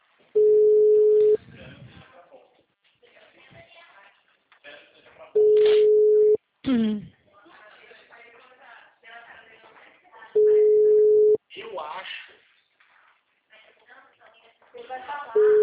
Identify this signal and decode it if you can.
Ringback tone (Europe, 425 Hz), with speech.
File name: out- -2017- - - .54